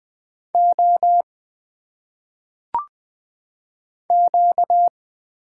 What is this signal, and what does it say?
Morse code, heard as a keyed tone, about 20 wpm, 700 Hz. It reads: O Q
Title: Single Letters - Phonetic Pronunciation 20wpm